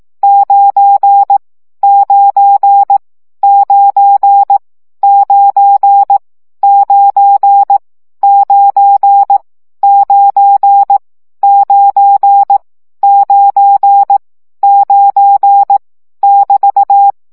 Morse code, heard as a keyed tone, about 18 wpm, 800 Hz. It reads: 9 9 9 9 9 9 9 9 9 9 <BT>
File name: CW gyakorlas 9 15 baud